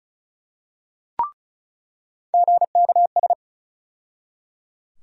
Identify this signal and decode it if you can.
Morse code, heard as a keyed tone, about 34 wpm, 700 Hz. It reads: GKS
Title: USA Call Sign Postfixes - 3 Letters 35wpm